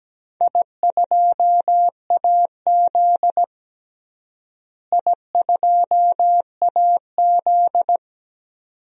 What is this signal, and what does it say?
Morse code, heard as a keyed tone, about 17 wpm, 700 Hz. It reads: I2AZ I2AZ
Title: DXCC-100 Call Signs - 4 Characters - Repeated 2x - 17wpm